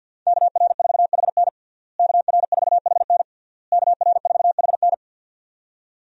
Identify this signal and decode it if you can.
Morse code, heard as a keyed tone, about 52 wpm, 700 Hz. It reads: KR4HN KR4HN KR4HN
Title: USA Call Signs - 5 Characters - Repeated 3x - 50wpm